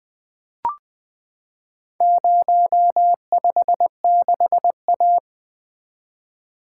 Morse code, heard as a keyed tone, about 20 wpm, 700 Hz. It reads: 056A
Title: Four Letter-Number Combinations - 20wpm